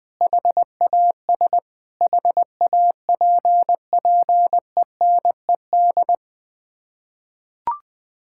Morse code, heard as a keyed tone, about 21 wpm, 700 Hz. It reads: HAS HAPPENED
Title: Sets of 8 Words - 20wpm